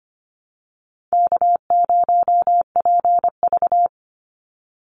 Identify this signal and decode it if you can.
Morse code, heard as a keyed tone, about 25 wpm, 700 Hz. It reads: K0PV
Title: Easy Call Signs - 25wpm